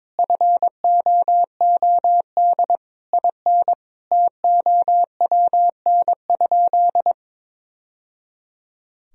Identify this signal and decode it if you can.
Morse code, heard as a keyed tone, about 22 wpm, 700 Hz. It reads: FOOD IN TOWN?
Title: Sentences from Top 1000 Words - Review Part 4 - 22wpm